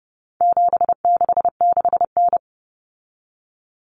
Morse code, heard as a keyed tone, about 30 wpm, 700 Hz. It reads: 766N
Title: Common Cut Numbers - 4-Digit 30wpm